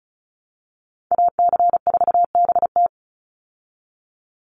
Morse code, heard as a keyed tone, about 34 wpm, 700 Hz. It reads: AC4BT
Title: Call Signs Repeated 3x - 35wpm